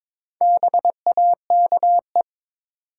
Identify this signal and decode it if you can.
Morse code, heard as a keyed tone, about 22 wpm, 700 Hz. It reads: BAKE
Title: Four Letter Words - 22wpm